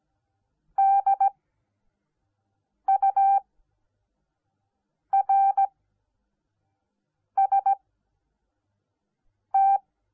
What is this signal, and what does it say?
Morse code, heard as a keyed tone, about 17 wpm, spaced as for 5 wpm, 780 Hz. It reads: DURST